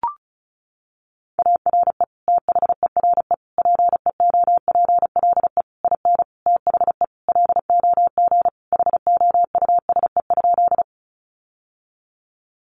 Morse code, heard as a keyed tone, about 34 wpm, 700 Hz. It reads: ARE THERE PEOPLE IN THE LOG HOUSE?
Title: Sentences from Top 1400 Words 35wpm